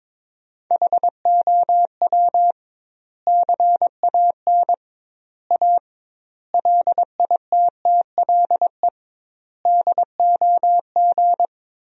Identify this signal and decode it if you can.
Morse code, heard as a keyed tone, about 22 wpm, 700 Hz. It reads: HOW CAN A LITTLE DOG
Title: 2x Word-Spacing - Sentences from Top 500 Words - 22wpm